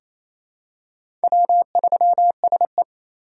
Morse code, heard as a keyed tone, about 28 wpm, 700 Hz. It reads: W3SE
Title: USA Call Signs - 8x Character Spacing - 28wpm